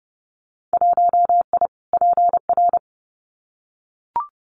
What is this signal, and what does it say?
Morse code, heard as a keyed tone, about 31 wpm, 700 Hz. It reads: 1I PR
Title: Winter Field Day Exchanges Encore 30wpm